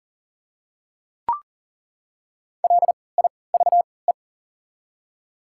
Morse code, heard as a keyed tone, about 40 wpm, spaced as for 23 wpm, 700 Hz. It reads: LIVE
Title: 3x Character Spacing - Top 200 Words - 40wpm